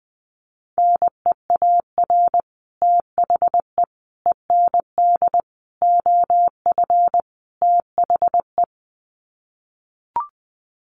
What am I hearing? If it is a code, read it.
Morse code, heard as a keyed tone, about 20 wpm, 700 Hz. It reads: NEAR THE END OF THE
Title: Sets of 5 Words Encore 20wpm